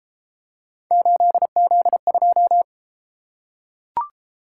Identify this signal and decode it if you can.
Morse code, heard as a keyed tone, about 30 wpm, 700 Hz. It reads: 8Z2